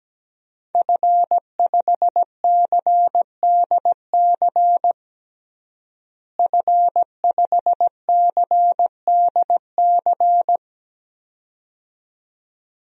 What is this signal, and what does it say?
Morse code, heard as a keyed tone, about 17 wpm, 700 Hz. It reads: F5CDC F5CDC
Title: DXCC-100 Call Signs - 5 Characters - Repeated 2x - 17wpm